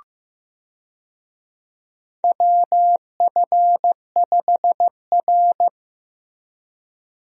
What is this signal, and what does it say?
Morse code, heard as a keyed tone, about 15 wpm, 700 Hz. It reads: WF5R
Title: USA Call Signs - 4 Characters - Repeated 3x - 15wpm